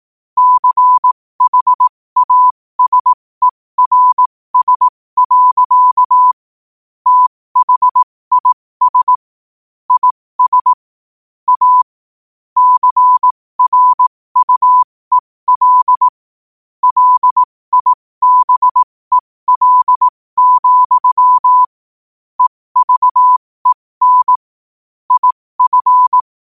Morse code, heard as a keyed tone, about 18 wpm, spaced as for 15 wpm, 1000 Hz. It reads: CHASERS. THIS IS A CRUEL LIBEL, EVEN IF
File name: qotdfiles / 15wpm